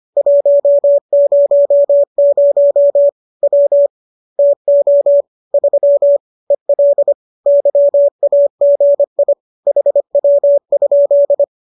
Morse code, heard as a keyed tone, about 25 wpm, 570 Hz. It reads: 100 W TO 3 EL YAGI HW?